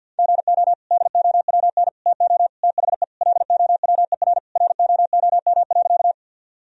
Morse code, heard as a keyed tone, about 50 wpm, 700 Hz. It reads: GO DOWN TO THE LOWER ROOM.
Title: Sentences from Top 1000 Words - Review Part 4 - 50wpm